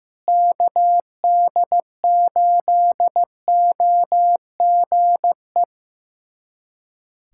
Morse code, heard as a keyed tone, about 15 wpm, 700 Hz. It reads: KD8OGE
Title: USA Callsign DE Callsign - 6 Characters - 15wpm